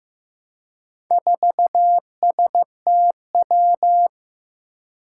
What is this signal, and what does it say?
Morse code, heard as a keyed tone, about 15 wpm, 700 Hz. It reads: 4STW
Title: Four Challenging Letter-Number Combinations 15wpm